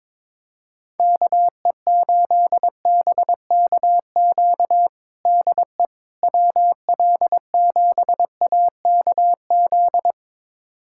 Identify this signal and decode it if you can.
Morse code, heard as a keyed tone, about 22 wpm, 700 Hz. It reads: KE8BKQ DE WL7AKZ
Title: USA Callsign DE Callsign - 6 Characters - 22wpm